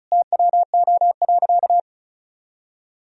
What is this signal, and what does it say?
Morse code, heard as a keyed tone, about 36 wpm, 700 Hz. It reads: TWO.
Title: Sentences from Top 200 Words and Abbreviations 35wpm